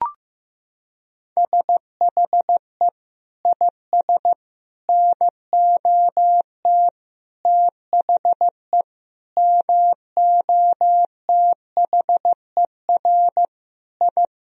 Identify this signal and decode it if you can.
Morse code, heard as a keyed tone, about 15 wpm, 700 Hz. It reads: SHE IS NOT THE MOTHER I
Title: Sentences from Top 1000 Words - Review Part 2 - 15wpm